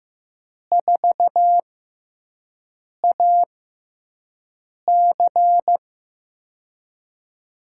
Morse code, heard as a keyed tone, about 15 wpm, spaced as for 5 wpm, 700 Hz. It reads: 4AC
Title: USA Call Signs - 5 Characters - 6x Character Spacing - 15wpm